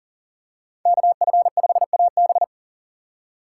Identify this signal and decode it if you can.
Morse code, heard as a keyed tone, about 38 wpm, 700 Hz. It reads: KF5AB
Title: USA Call Signs - 2x Character Spacing - 40wpm